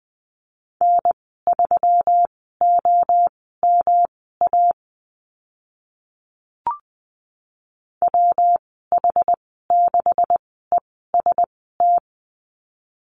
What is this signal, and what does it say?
Morse code, heard as a keyed tone, about 20 wpm, spaced as for 14 wpm, 700 Hz. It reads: N3OMA WH6EST
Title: 147 Lesson 24g - Letter y - Call Signs 20wpm